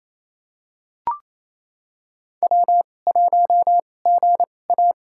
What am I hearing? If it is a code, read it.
Morse code, heard as a keyed tone, about 28 wpm, spaced as for 20 wpm, 700 Hz. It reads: W1GA
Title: USA Call Signs - 2x Character Spacing - 28wpm